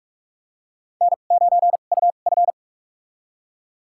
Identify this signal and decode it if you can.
Morse code, heard as a keyed tone, about 46 wpm, spaced as for 33 wpm, 700 Hz. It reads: N9UF
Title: USA Call Signs - 4 Characters - 2x Character Spacing - 45wpm